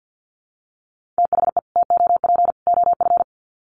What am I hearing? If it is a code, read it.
Morse code, heard as a keyed tone, about 49 wpm, 700 Hz. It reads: THE TOP OF
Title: Sets of 3 Words 50wpm